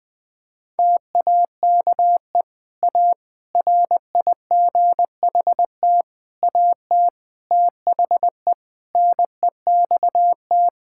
Morse code, heard as a keyed tone, about 20 wpm, 700 Hz. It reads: TAKE A RIGHT AT THE NEXT
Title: Sentences from Top 1600 Words 20wpm